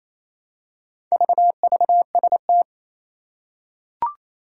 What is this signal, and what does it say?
Morse code, heard as a keyed tone, about 27 wpm, 700 Hz. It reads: VVST
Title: Four Challenging Letter-Number Combinations 28wpm